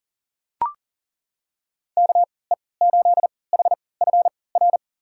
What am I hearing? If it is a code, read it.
Morse code, heard as a keyed tone, about 38 wpm, spaced as for 22 wpm, 700 Hz. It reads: KE8HFR
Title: USA Call Signs - 6 Characters - 3x Character Spacing - 40wpm